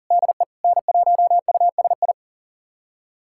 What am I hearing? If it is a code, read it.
Morse code, heard as a keyed tone, about 40 wpm, 690 Hz. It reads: DE N1USI
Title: USA Callsign DE Callsign - 5 Characters - 40wpm